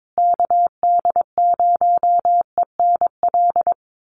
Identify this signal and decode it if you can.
Morse code, heard as a keyed tone, about 22 wpm, 700 Hz. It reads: KD0ENL
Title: USA Call Signs Repeated 3x 22wpm